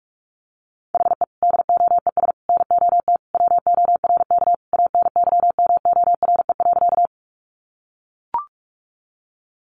Morse code, heard as a keyed tone, about 44 wpm, 700 Hz. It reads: HE DOES NOT WORK ANYMORE.
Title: Additional Easy Phrases 45wpm